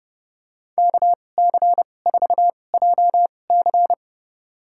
Morse code, heard as a keyed tone, about 30 wpm, spaced as for 22 wpm, 700 Hz. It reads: KC4JC
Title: USA Call Signs - 5 Characters - 2x Character Spacing - 30wpm